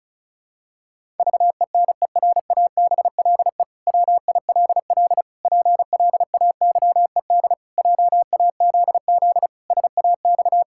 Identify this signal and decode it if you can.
Morse code, heard as a keyed tone, about 36 wpm, 700 Hz. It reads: VENERABLE WILL PLAYED JAZZ SAX
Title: Pangram Challenge v2 - 35wpm